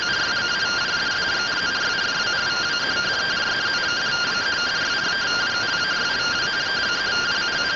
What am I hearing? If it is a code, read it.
Two-tone teleprinter signal, about 170 Hz shift.